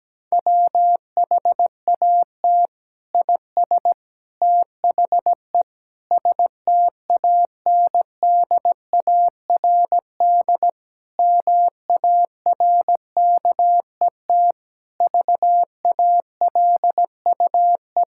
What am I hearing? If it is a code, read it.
Morse code, heard as a keyed tone, about 17 wpm, 700 Hz. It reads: WHAT IS THE STANDARD MARKET VALUE
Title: Sentences from Top 1000 Words - Review Part 6 - 17wpm